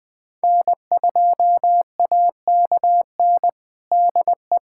Morse code, heard as a keyed tone, about 20 wpm, 700 Hz. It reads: N2AKN DE
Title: USA Callsign DE Callsign - 5 Characters - 20wpm